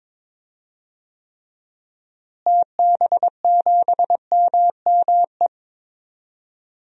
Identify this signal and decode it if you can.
Morse code, heard as a keyed tone, about 22 wpm, 700 Hz. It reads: TB7MME
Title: DXCC-100 Call Signs - 6 Characters - Repeated 3x - 22wpm